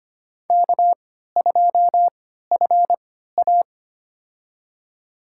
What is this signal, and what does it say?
Morse code, heard as a keyed tone, about 25 wpm, spaced as for 14 wpm, 700 Hz. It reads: K2FA
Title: USA Call Signs - 4 Characters - 3x Character Spacing - 25wpm